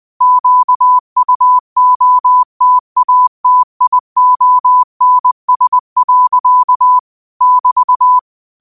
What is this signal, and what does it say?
Morse code, heard as a keyed tone, about 20 wpm, 1000 Hz. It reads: QUOTATIONS. <BT>
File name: 20wpm